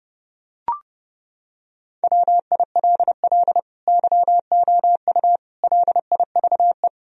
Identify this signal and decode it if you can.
Morse code, heard as a keyed tone, about 29 wpm, 700 Hz. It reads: WILL YOU LIVE
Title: Sentences from Top 1000 Words - Review Part 3 - 30wpm